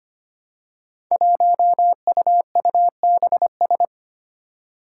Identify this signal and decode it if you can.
Morse code, heard as a keyed tone, about 25 wpm, 700 Hz. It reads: 1UUBS
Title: Five Challenging Letter-Number Combinations 25wpm